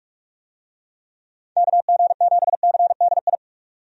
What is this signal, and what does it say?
Morse code, heard as a keyed tone, about 41 wpm, 700 Hz. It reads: KG7CDI
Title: USA Call Signs - 6 Characters - 45wpm